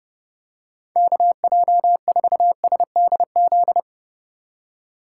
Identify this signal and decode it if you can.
Morse code, heard as a keyed tone, about 30 wpm, 700 Hz. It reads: KJ4SDZ